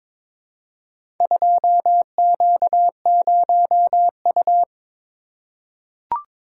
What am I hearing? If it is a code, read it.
Morse code, heard as a keyed tone, about 23 wpm, 700 Hz. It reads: 2Q0U
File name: practice / 22wpm / Four Letter-Number Combinations - 22wpm